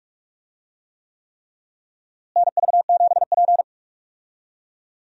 Morse code, heard as a keyed tone, about 46 wpm, 700 Hz. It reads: NV7P
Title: USA Call Signs - 4 Characters - Repeated 3x - 45wpm